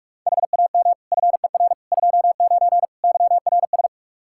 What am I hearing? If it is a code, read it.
Morse code, heard as a keyed tone, about 46 wpm, 700 Hz. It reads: HAM FER 29 YRS
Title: K1USN SST QSO Practice 45wpm